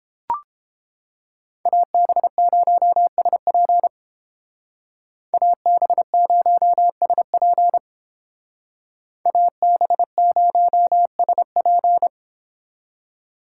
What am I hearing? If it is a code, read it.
Morse code, heard as a keyed tone, about 30 wpm, 700 Hz. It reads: AB0SP AB0SP AB0SP